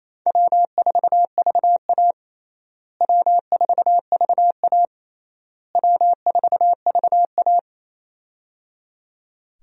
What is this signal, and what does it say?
Morse code, heard as a keyed tone, about 28 wpm, 700 Hz. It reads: W4VA W4VA W4VA
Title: USA Call Signs - 4 Characters - Repeated 3x - 28wpm